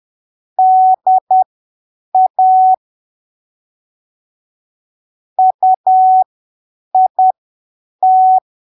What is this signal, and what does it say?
Morse code, heard as a keyed tone, about 10 wpm, spaced as for 7 wpm, 750 Hz. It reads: DA UIT
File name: arquivos / texto / texto 12 10ppm